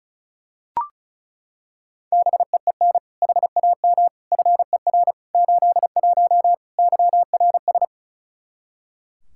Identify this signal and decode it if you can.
Morse code, heard as a keyed tone, about 36 wpm, 700 Hz. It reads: BEEN HAM FER 81 YRS
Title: Practice QSO - Exchange 3 - More Info - 35wpm